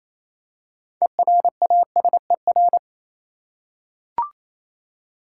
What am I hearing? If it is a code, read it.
Morse code, heard as a keyed tone, about 27 wpm, 700 Hz. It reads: ERASER
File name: Easy Words - Speed-Racing with Hints - 22wpm